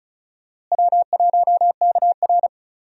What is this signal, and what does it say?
Morse code, heard as a keyed tone, about 34 wpm, 700 Hz. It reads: W1KR
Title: USA Call Signs - 4 Characters - Repeated 3x - 35wpm